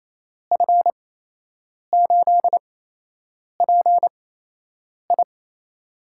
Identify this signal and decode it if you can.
Morse code, heard as a keyed tone, about 28 wpm, spaced as for 8 wpm, 700 Hz. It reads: F8PI
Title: USA Call Signs - 5 Characters - 8x Character Spacing - 28wpm